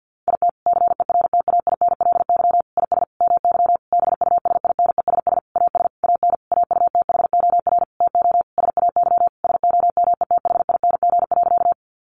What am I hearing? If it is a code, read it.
Morse code, heard as a keyed tone, about 50 wpm, 700 Hz. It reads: IT CERTAINLY IS MY BUSINESS AS AN AUTHOR TO SAY SOMETHING.
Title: Sentences from Top 1200 Words 50wpm